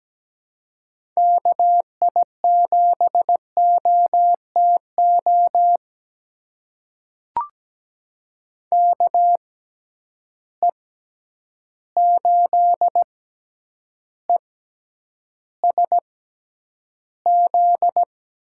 Morse code, heard as a keyed tone, about 17 wpm, spaced as for 9 wpm, 700 Hz. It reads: KI7OTO KE8ESZ
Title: USA Call Signs - 6x Character Spacing - 17wpm